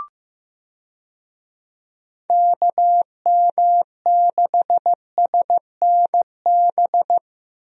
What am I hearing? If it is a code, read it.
Morse code, heard as a keyed tone, about 15 wpm, 700 Hz. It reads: KM6SNB
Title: USA Call Signs - 6 Characters - Repeated 2x - 15wpm